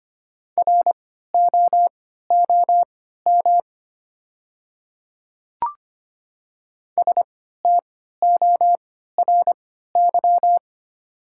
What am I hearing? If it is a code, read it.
Morse code, heard as a keyed tone, about 25 wpm, spaced as for 14 wpm, 700 Hz. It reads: ROOM STORY
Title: 3x Character Spacing - Top 500 Words - 25wpm